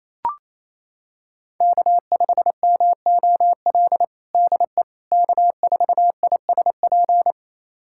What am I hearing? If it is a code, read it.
Morse code, heard as a keyed tone, about 28 wpm, 700 Hz. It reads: K5MOL DE K4ISP
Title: USA Callsign DE Callsign - 5 Characters - 28wpm